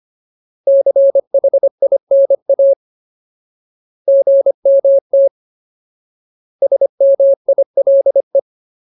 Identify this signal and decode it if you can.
Morse code, heard as a keyed tone, about 25 wpm, 550 Hz. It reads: CHINA GMT SMILE
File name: Ultimate Word List / 25WPM Ultimate